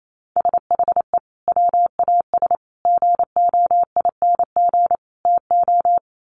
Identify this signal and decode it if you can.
Morse code, heard as a keyed tone, about 28 wpm, 700 Hz. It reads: SHE WAS GOING TO
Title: Sets of 4 Words 28wpm